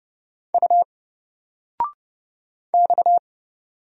Morse code, heard as a keyed tone, about 30 wpm, 700 Hz. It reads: U X
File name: Single-Letter-Number-30wpm